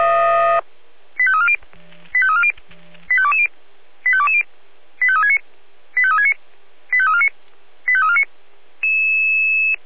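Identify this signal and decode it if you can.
Five-tone selective call (ZVEI), heard as quick runs of five tones, tones of about 70 ms. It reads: 85340 85340 85220 85220 85369 85369 85349 85349
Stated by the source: ZVEI five-tone call 85279 received on an SDR.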